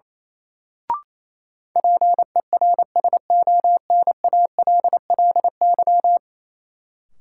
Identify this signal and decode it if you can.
Morse code, heard as a keyed tone, about 28 wpm, 700 Hz. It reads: PERSONALLY
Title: Sentences from Top 1400 Words - New Word Drill Alt - 28wpm